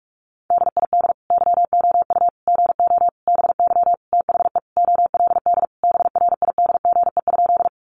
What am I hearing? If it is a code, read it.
Morse code, heard as a keyed tone, about 44 wpm, 700 Hz. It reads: DID YOU GO BY THE OLD BRIDGE?
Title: Sentences from Top 1200 Words 45wpm